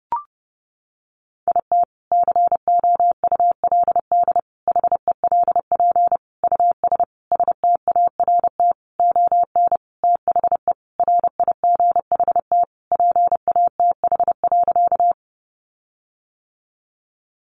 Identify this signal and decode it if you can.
Morse code, heard as a keyed tone, about 30 wpm, 700 Hz. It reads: IT COULD HELP US START ON THE RIGHT PATH.
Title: Sentences from Top 1400 Words 30wpm